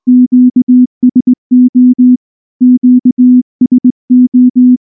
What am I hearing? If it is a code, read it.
Morse code, heard as a keyed tone, about 19 wpm, 250 Hz. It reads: QSO QSO
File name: morse / morse-training-20WPM-9